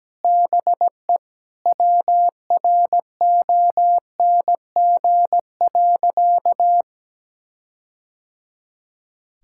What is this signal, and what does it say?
Morse code, heard as a keyed tone, about 17 wpm, 700 Hz. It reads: BE WRONG.
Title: Sentences from Top 1600 Words 17wpm